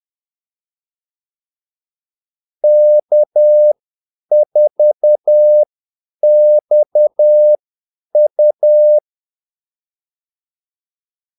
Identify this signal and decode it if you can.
Morse code, heard as a keyed tone, about 10 wpm, spaced as for 8 wpm, 600 Hz. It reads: K4XU
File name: Callsigns-10-8wpm